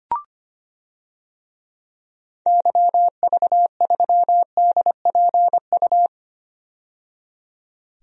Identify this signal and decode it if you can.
Morse code, heard as a keyed tone, about 25 wpm, 690 Hz. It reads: YV3DPU
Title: DXCC-100 Call Signs - 6 Characters - 25wpm